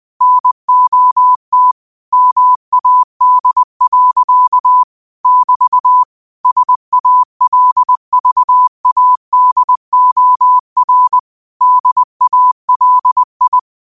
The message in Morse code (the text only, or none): NOT MAD. <BT> SALVADOR DALI